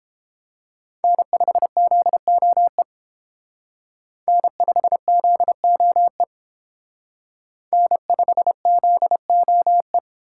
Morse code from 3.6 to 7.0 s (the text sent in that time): N5ZOE